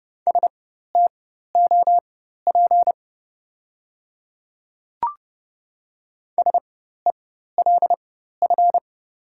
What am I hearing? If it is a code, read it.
Morse code, heard as a keyed tone, about 32 wpm, spaced as for 14 wpm, 700 Hz. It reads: STOP SELF
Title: 4x Character Spacing - Top 300 Words - 30wpm